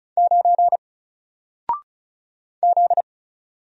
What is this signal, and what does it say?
Morse code, heard as a keyed tone, about 36 wpm, 700 Hz. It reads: 9 Z
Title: Single Letter-Number 35wpm